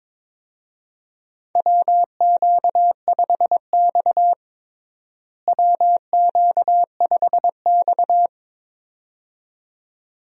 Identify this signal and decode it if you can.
Morse code, heard as a keyed tone, about 21 wpm, 700 Hz. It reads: WQ5X WQ5X
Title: USA Call Signs - 4 Characters - Repeated 2x - 22wpm